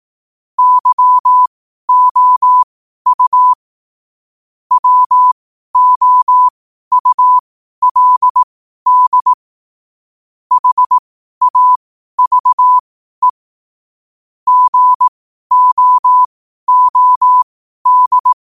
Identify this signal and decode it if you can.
Morse code, heard as a keyed tone, about 18 wpm, spaced as for 12 wpm, 1000 Hz. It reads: YOU WOULD HAVE GOOD
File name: qotdfiles / 12wpm